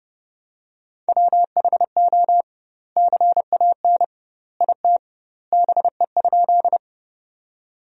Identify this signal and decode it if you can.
Morse code, heard as a keyed tone, about 30 wpm, 700 Hz. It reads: WHO CAN IT BE?
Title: 2x Word-Spacing - Sentences from Top 100 Words - 30wpm